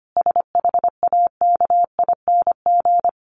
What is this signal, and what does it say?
Morse code, heard as a keyed tone, about 25 wpm, 700 Hz. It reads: SHAKING
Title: A Sound Of Thunder 25wpm